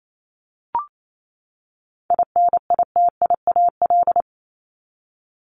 Morse code, heard as a keyed tone, about 28 wpm, 700 Hz. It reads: INITIAL
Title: Top 1501-1750 Words 28wpm